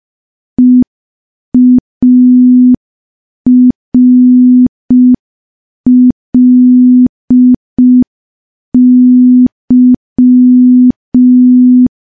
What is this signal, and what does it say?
Morse code, heard as a keyed tone, about 5 wpm, 250 Hz. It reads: EARLY
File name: morse / morse-training-05WPM